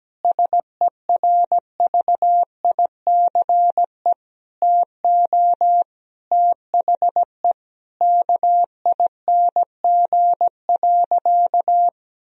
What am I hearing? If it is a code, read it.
Morse code, heard as a keyed tone, about 17 wpm, 700 Hz. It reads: SERVICE TO THE KING.
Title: Sentences from Top 1000 Words - Review Part 4 - 17wpm